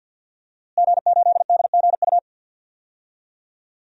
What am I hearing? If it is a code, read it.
Morse code, heard as a keyed tone, about 51 wpm, 700 Hz. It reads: G8DGU
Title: DXCC-100 Call Signs - 5 Characters - Repeated 3x - 50wpm